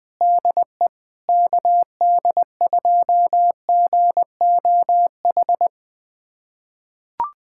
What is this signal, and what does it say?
Morse code, heard as a keyed tone, about 20 wpm, 700 Hz. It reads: DE KD2GOH
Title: USA Callsign DE Callsign 20wpm